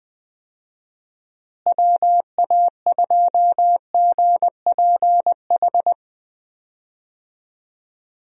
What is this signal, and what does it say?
Morse code, heard as a keyed tone, about 21 wpm, 700 Hz. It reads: WA2GPH
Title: USA Call Signs - 6 Characters - Repeated 3x - 20wpm